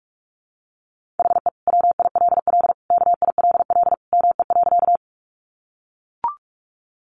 Morse code, heard as a keyed tone, about 44 wpm, 700 Hz. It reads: HE WILL KILL ME.
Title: Sentences from Top 1400 Words 45wpm